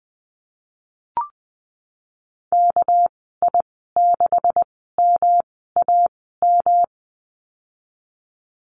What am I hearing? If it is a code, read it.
Morse code, heard as a keyed tone, about 20 wpm, spaced as for 15 wpm, 700 Hz. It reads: KI6MAM
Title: 207 Lesson 33g - Letter j - Call Signs 20wpm